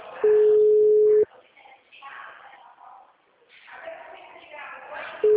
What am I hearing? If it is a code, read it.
Ringback tone (Europe, 425 Hz), with speech.
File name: out- -2045- - - .39013